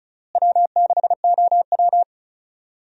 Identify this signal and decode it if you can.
Morse code, heard as a keyed tone, about 36 wpm, 700 Hz. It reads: W6OW
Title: USA Call Signs - 4 Characters - Repeated 3x - 35wpm